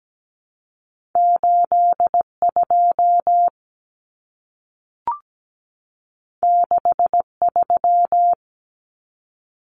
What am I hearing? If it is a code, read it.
Morse code, heard as a keyed tone, about 17 wpm, 700 Hz. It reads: 8 2 6 3
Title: Common Cut Numbers - 2-Digit 17wpm